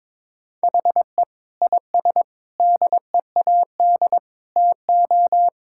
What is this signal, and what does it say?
Morse code, heard as a keyed tone, about 22 wpm, 700 Hz. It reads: HE IS DEAD TO
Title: Sentences from Top 1600 Words 22wpm